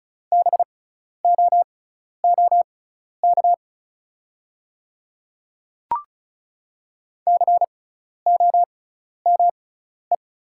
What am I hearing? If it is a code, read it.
Morse code, heard as a keyed tone, about 34 wpm, spaced as for 12 wpm, 690 Hz. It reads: BOOK COME
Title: 6x Character Spacing - Top 500 Words - 35wpm